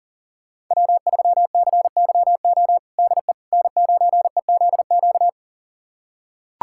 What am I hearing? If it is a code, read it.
Morse code, heard as a keyed tone, about 40 wpm, 700 Hz. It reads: W3CYO DE N9EZQ
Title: USA Callsign DE Callsign - 5 Characters - 40wpm